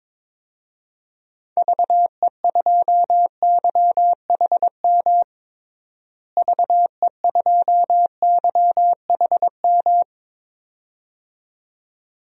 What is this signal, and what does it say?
Morse code, heard as a keyed tone, about 22 wpm, 700 Hz. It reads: VE2YHM VE2YHM
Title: DXCC-100 Call Signs - 6 Characters - Repeated 2x - 22wpm